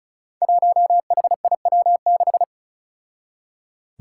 Morse code, heard as a keyed tone, about 36 wpm, 700 Hz. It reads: 1HIW6
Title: Five Challenging Letter-Number Combinations 35wpm